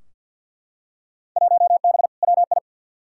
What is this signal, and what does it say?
Morse code, heard as a keyed tone, about 52 wpm, 700 Hz. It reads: 1B WI
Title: Field Day Exchanges - 50wpm